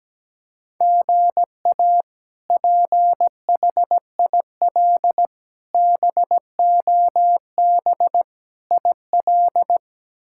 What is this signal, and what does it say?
Morse code, heard as a keyed tone, about 17 wpm, 700 Hz. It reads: GA PHIL BOB IL
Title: K1USN SST QSO Practice - 17wpm